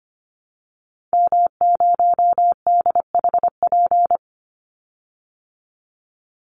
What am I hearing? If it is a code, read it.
Morse code, heard as a keyed tone, about 25 wpm, 700 Hz. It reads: M0DHP
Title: Call Signs Repeated 2x - 25wpm